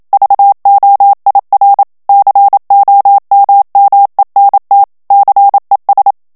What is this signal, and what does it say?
Morse code, heard as a keyed tone, about 27 wpm, 800 Hz. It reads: VOIR COMMENT CES
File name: FRANCAIS TEXTE CLAIR 2 VITESSE 26 MOTS MINUTES